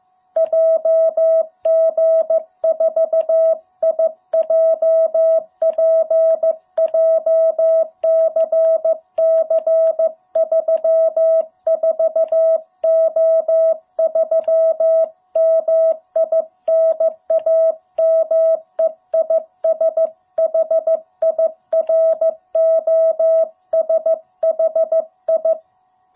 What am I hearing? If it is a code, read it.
Morse code, heard as a keyed tone, about 15 wpm, 660 Hz. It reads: JG4IJP JCC34O3 MI NAME IS HIROSHI
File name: jh8cbh / 2021cw